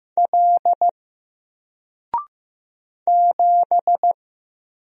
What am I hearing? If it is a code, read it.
Morse code, heard as a keyed tone, about 15 wpm, 700 Hz. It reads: L 7
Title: Single Letter-Number Alt 15wpm